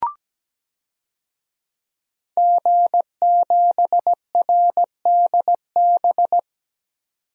Morse code, heard as a keyed tone, about 17 wpm, 700 Hz. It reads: G7RDB
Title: DXCC-100 Call Signs - 5 Characters - 17wpm